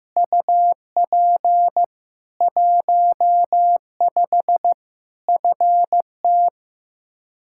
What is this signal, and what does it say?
Morse code, heard as a keyed tone, about 15 wpm, 700 Hz. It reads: UP 15 FT
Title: Practice QSO - Exchange 2 - Info - 15wpm